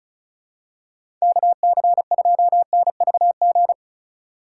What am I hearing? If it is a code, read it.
Morse code, heard as a keyed tone, about 36 wpm, 700 Hz. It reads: KC2NVG
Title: USA Call Signs - 6 Characters - 3x Character Spacing - 35wpm